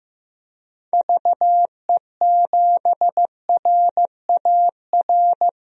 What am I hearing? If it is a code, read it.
Morse code, heard as a keyed tone, about 15 wpm, 700 Hz. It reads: VE7RAR
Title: DXCC-100 Call Signs - 6 Characters - 15wpm